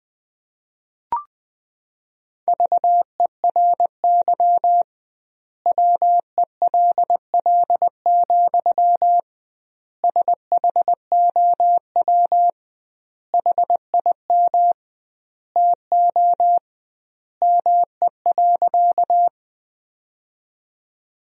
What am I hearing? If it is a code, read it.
Morse code, heard as a keyed tone, about 21 wpm, 700 Hz. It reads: VERY WELL, SHOW HIM TO ME.
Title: 2x Word-Spacing - Sentences from Top 200 Words - 20wpm